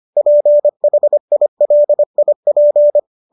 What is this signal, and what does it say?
Morse code, heard as a keyed tone, about 25 wpm, 590 Hz. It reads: PHILIP